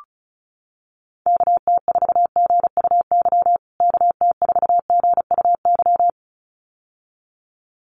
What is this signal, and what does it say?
Morse code, heard as a keyed tone, about 34 wpm, 700 Hz. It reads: KT4GUY KT4GUY
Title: USA Call Signs Repeated 2x 35wpm